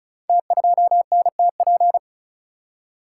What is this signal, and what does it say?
Morse code, heard as a keyed tone, about 35 wpm, 690 Hz. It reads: T2NTP
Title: Code Groups - 5 Letter:Numbers - 35wpm